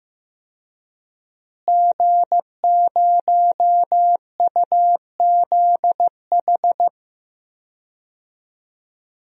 Morse code, heard as a keyed tone, about 15 wpm, 700 Hz. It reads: G0UZH
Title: DXCC-100 Call Signs - 5 Characters - Repeated 3x - 15wpm